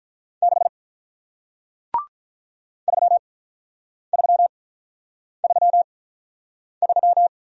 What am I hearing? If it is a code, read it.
Morse code, heard as a keyed tone, about 46 wpm, spaced as for 9 wpm, 700 Hz. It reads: 6 3333